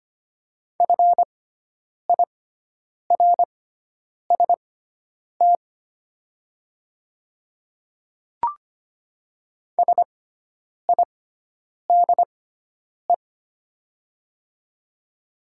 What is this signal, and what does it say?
Morse code, heard as a keyed tone, about 25 wpm, spaced as for 9 wpm, 690 Hz. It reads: FIRST SIDE
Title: 6x Character Spacing - Top 500 Words - 25wpm